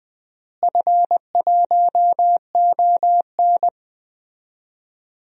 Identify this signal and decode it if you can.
Morse code, heard as a keyed tone, about 20 wpm, 700 Hz. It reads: F1ON
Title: DXCC-100 Call Signs - 4 Characters - Repeated 3x - 20wpm